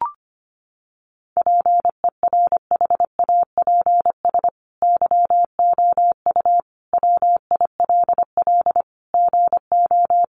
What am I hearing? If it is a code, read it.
Morse code, heard as a keyed tone, about 25 wpm, 700 Hz. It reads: PERHAPS YOU WILL GO
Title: Sentences from Top 1300 Words 25wpm